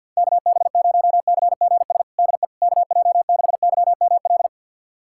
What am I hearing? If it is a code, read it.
Morse code, heard as a keyed tone, about 51 wpm, 700 Hz. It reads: KB0CGS DE KJ6YML
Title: USA Callsign DE Callsign - 6 Characters - 50wpm